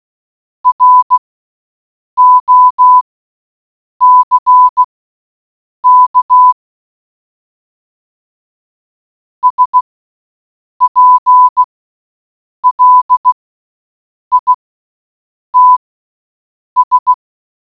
Morse code, heard as a keyed tone, about 16 wpm, spaced as for 7 wpm, 1000 Hz. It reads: ROCK SPLITS